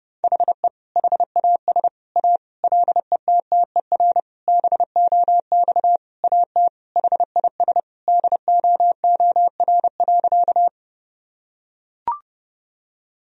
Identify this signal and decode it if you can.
Morse code, heard as a keyed tone, about 30 wpm, 700 Hz. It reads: HE HAS A LETTER BOX AT HIS DOOR.
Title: Sentences from Top 1000 Words - Review Part 2 - 30wpm